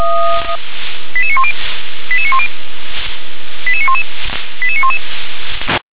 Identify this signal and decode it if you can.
Five-tone selective call (ZVEI), heard as quick runs of five tones, tones of about 70 ms. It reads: 88910 88910 88911 88911
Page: SDR/ZVEI